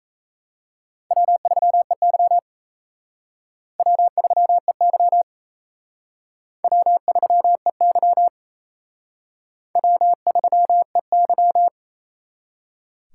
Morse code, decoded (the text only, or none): W3EY W3EY W3EY W3EY